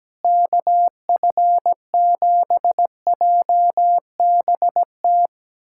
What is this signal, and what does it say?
Morse code, heard as a keyed tone, about 17 wpm, 700 Hz. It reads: KF7JBT
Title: USA Callsign DE Callsign 17wpm